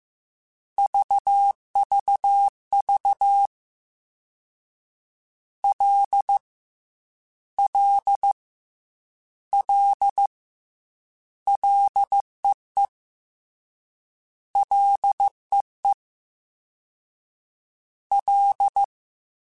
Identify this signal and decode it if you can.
Morse code, heard as a keyed tone, about 15 wpm, 800 Hz. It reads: VVV L L L LEE LEE L
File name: F and L / F and L mp3 / F and L Prefixes 3x2 Twice @13